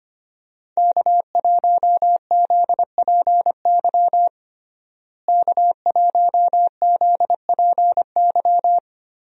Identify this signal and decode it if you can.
Morse code, heard as a keyed tone, about 25 wpm, 700 Hz. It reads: K1ZPY K1ZPY
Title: USA Call Signs - 5 Characters - Repeated 3x - 25wpm